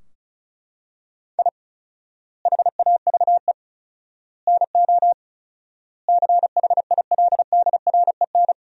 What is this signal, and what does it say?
Morse code, heard as a keyed tone, about 36 wpm, 700 Hz. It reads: I HAVE NO CHILDREN
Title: 4x Word-Spacing - Sentences from Top 400 Words - 35wpm